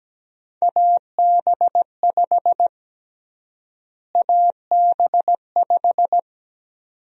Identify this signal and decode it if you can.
Morse code, heard as a keyed tone, about 17 wpm, 700 Hz. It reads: AB5 AB5
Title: USA Call Sign Prefixes - Repeated 2x - 17wpm